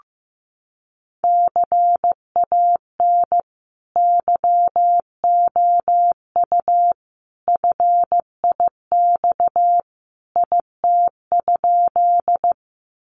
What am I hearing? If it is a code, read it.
Morse code, heard as a keyed tone, about 15 wpm, 700 Hz. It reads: CAN YOU FIX IT?
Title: Sentences from Top 1400 Words 15wpm